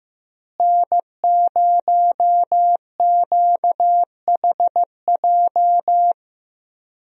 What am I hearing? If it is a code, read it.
Morse code, heard as a keyed tone, about 15 wpm, 700 Hz. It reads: N0QHJ
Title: USA Call Signs - 5 Characters - Repeated 2x - 15wpm